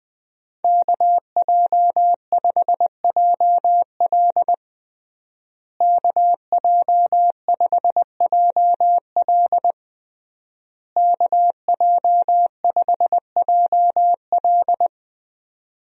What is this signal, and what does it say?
Morse code, heard as a keyed tone, about 20 wpm, 700 Hz. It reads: KJ5JL KJ5JL KJ5JL
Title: USA Call Signs - 5 Characters - Repeated 3x - 20wpm